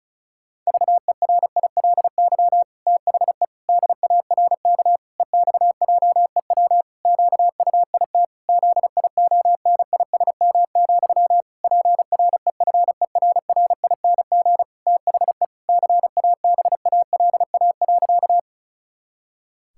Morse code, heard as a keyed tone, about 36 wpm, 700 Hz. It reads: VERILY THE DARK EXJEW QUIT ZIONISM, PREFERRING THE CABALA.
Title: Pangram Challenge v2 - 35wpm